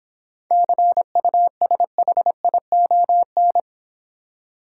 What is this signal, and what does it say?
Morse code, heard as a keyed tone, about 25 wpm, 700 Hz. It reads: CUSHION
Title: Easy Words - Speed-Racing with Hints - 20wpm